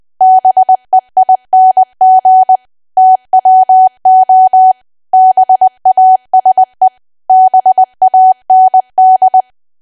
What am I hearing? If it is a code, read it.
Morse code, heard as a keyed tone, about 20 wpm, 750 Hz. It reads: BEING TWO BASE BAND
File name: cwa / W1AW-Level3 / 20WPM